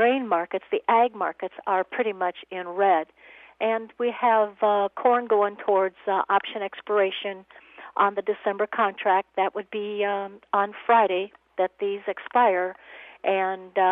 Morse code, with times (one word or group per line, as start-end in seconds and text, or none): none